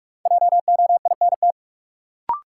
Morse code, heard as a keyed tone, about 41 wpm, 700 Hz. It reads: JOINT
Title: Sentences from Top 1300 Words - New Word Drill Alt - 45wpm